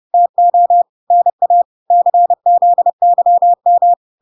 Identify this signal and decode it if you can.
Morse code, heard as a keyed tone, about 30 wpm, 700 Hz. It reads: TO NA CZYM